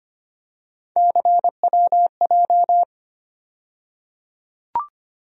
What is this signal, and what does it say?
Morse code, heard as a keyed tone, about 25 wpm, 700 Hz. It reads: CWJ
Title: USA Call Sign Postfixes - 3 Letters 25wpm